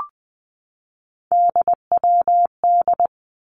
Morse code, heard as a keyed tone, about 20 wpm, 700 Hz. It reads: DWD
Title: Three Challenging Letter-Number Combinations 20wpm